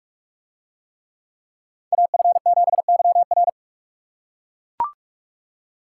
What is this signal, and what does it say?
Morse code, heard as a keyed tone, about 44 wpm, 700 Hz. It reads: AF7YR